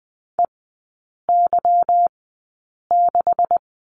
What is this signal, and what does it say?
Morse code, heard as a keyed tone, about 20 wpm, spaced as for 8 wpm, 700 Hz. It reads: EY6
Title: 145 Lesson 24e - Letter y - Three Characters 20wpm